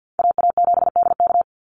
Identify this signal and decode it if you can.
Morse code, heard as a keyed tone, about 48 wpm, 700 Hz. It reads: AA7DK
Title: NA SOTA Chaser Callsigns 50wpm